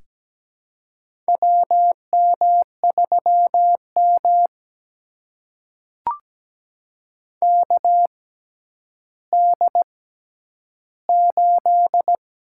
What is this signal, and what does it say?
Morse code, heard as a keyed tone, about 17 wpm, spaced as for 10 wpm, 700 Hz. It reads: WM3M KD8
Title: USA Call Signs - 6x Character Spacing - 17wpm